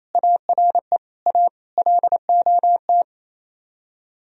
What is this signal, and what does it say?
Morse code, heard as a keyed tone, about 32 wpm, 700 Hz. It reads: ARE A LOT
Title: Speed-Racing - Sets of 4 Words 28wpm